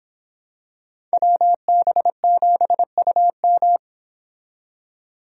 Morse code, heard as a keyed tone, about 25 wpm, 700 Hz. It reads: WB7UM